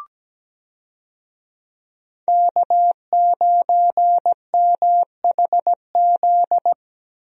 Morse code, heard as a keyed tone, about 17 wpm, 700 Hz. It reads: K9MHZ
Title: USA Call Signs - 5 Characters - Repeated 2x - 17wpm